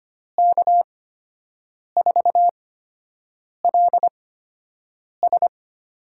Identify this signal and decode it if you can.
Morse code, heard as a keyed tone, about 25 wpm, spaced as for 7 wpm, 700 Hz. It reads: K4LS
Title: USA Call Signs - 8x Character Spacing - 25wpm